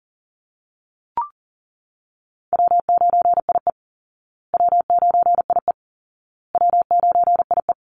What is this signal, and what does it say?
Morse code, heard as a keyed tone, about 40 wpm, 700 Hz. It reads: W9IE W9IE W9IE
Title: Easy Call Signs Repeated 3x - 40wpm